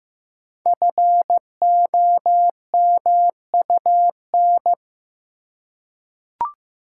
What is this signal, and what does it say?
Morse code, heard as a keyed tone, about 15 wpm, 700 Hz. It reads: FOMUN
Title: Code Groups - 5 Letters - 15wpm